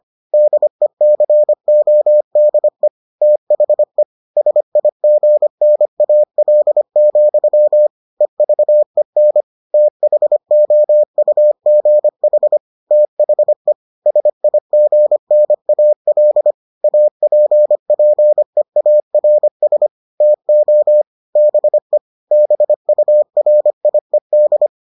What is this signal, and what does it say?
Morse code, heard as a keyed tone, about 25 wpm, 600 Hz. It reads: DECODE THE SIGNAL, EVEN THOUGH THE SIGNAL APPEARS TO BE BURIED